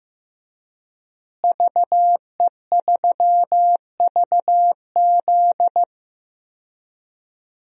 Morse code, heard as a keyed tone, about 15 wpm, 700 Hz. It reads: VE3VZ
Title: DXCC-100 Call Signs - 5 Characters - Repeated 2x - 15wpm